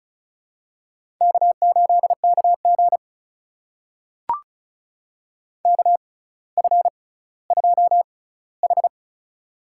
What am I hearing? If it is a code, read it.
Morse code, heard as a keyed tone, about 36 wpm, spaced as for 18 wpm, 700 Hz. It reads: K8KG KF2H
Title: USA Call Signs - 4 Characters - 6x Character Spacing - 35wpm